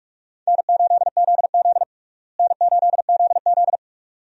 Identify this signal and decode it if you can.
Morse code, heard as a keyed tone, about 46 wpm, 700 Hz. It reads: N8ZZ N8ZZ
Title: USA Call Signs - 4 Characters - Repeated 2x - 45wpm